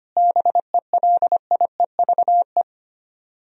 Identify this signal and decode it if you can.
Morse code, heard as a keyed tone, about 32 wpm, 700 Hz. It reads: BELIEVE